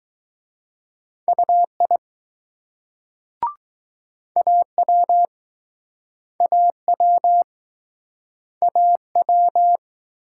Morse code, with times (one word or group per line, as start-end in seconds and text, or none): 1.3-2.0 UI
4.4-5.3 AW
6.4-7.4 AW
8.6-9.8 AW